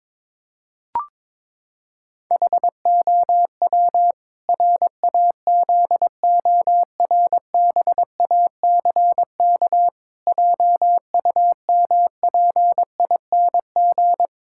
Morse code, heard as a keyed tone, about 22 wpm, 700 Hz. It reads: HOW RAZORBACK JUMPING
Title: Pangram Challenge v2 - 22wpm